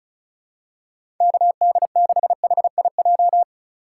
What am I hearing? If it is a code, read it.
Morse code, heard as a keyed tone, about 36 wpm, 700 Hz. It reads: KD6HIJ
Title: USA Call Signs - 6 Characters - 2x Character Spacing - 35wpm